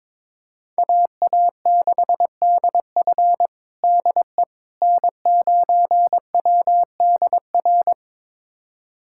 Morse code, heard as a keyed tone, about 22 wpm, 700 Hz. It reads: AA6DF DE N9WDR
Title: USA Callsign DE Callsign - 5 Characters - 22wpm